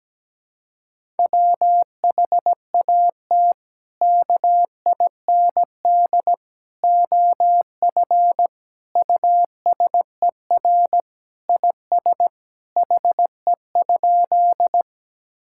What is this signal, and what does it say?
Morse code, heard as a keyed tone, about 17 wpm, 700 Hz. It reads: WHAT KIND OF USER IS HE?
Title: Sentences from Top 1000 Words - Review Part 5 - 17wpm